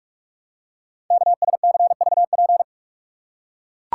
Morse code, heard as a keyed tone, about 44 wpm, 700 Hz. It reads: KSCVP
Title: Code Groups - 5 Letters - 45wpm